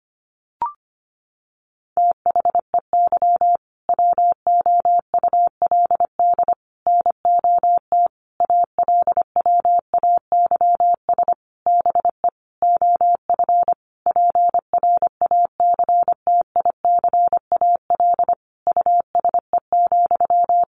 Morse code, heard as a keyed tone, about 25 wpm, 700 Hz. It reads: THEY WOULD NOT ALWAYS BE OF PRACTICAL USE,
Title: Sentences from Top 1400 Words 25wpm